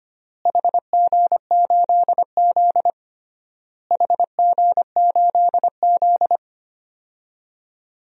Morse code, heard as a keyed tone, about 25 wpm, 700 Hz. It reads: HG8Z HG8Z
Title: DXCC-100 Call Signs - 4 Characters - Repeated 3x - 25wpm